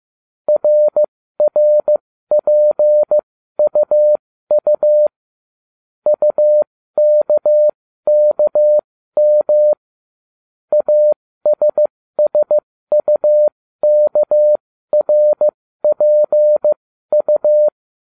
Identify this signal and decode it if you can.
Morse code, heard as a keyed tone, about 15 wpm, spaced as for 13 wpm, 600 Hz. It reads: RRPUU UKKM ASSUKRPU